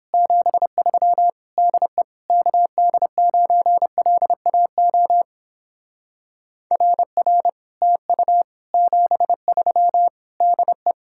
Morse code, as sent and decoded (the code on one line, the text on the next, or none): --... ...-- / -.. / -.- -.. ----. .-.. .- --- / .-. .-. / - ..- / --... ...-- / -.. .
73 DE KD9LAO RR TU 73 DE